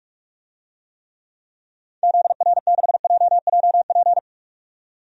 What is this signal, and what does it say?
Morse code, heard as a keyed tone, about 46 wpm, 700 Hz. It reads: ZR6JJP
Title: DXCC-100 Call Signs - 6 Characters - 45wpm